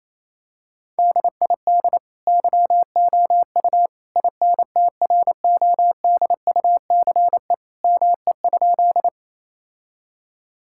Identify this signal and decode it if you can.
Morse code, heard as a keyed tone, about 28 wpm, 700 Hz. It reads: DID YOU INTRODUCE ME?
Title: Sentences from Top 1600 Words 28wpm